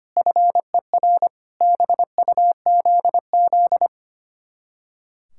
Morse code, heard as a keyed tone, about 30 wpm, 700 Hz. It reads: FER BUZZ
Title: Speed-Racing Common QSO Phrases - 25wpm